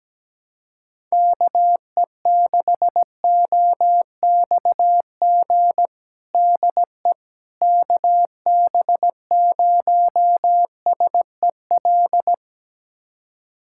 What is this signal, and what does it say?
Morse code, heard as a keyed tone, about 17 wpm, 700 Hz. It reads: KE6OXG DE KB0SEL
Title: USA Callsign DE Callsign - 6 Characters - 17wpm